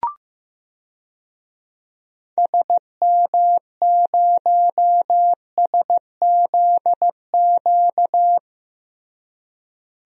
Morse code, heard as a keyed tone, about 15 wpm, 700 Hz. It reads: SM0SZQ
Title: DXCC-100 Call Signs - 6 Characters - Repeated 3x - 15wpm